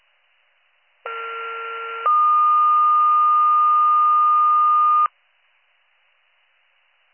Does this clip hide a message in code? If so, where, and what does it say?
Yes: two-tone page from 1.1 to 5.1 s, heard as a one-second tone, then a three-second tone, about 1370 then 1220 Hz.